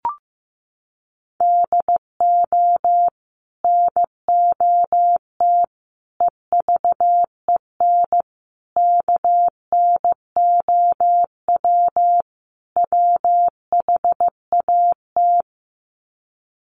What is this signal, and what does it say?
Morse code, heard as a keyed tone, about 15 wpm, 700 Hz. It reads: DO NOT EVEN KNOW WHAT
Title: Sets of 5 Words Encore 15wpm